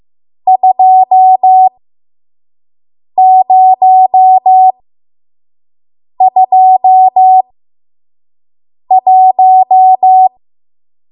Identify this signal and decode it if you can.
Morse code, heard as a keyed tone, about 15 wpm, spaced as for 5 wpm, 750 Hz. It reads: 2021
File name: Morse / 05WPM